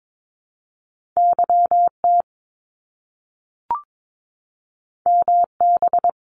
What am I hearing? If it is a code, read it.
Morse code, heard as a keyed tone, about 22 wpm, 700 Hz. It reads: YT MB
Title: Canadian Provinces and Territories 22wpm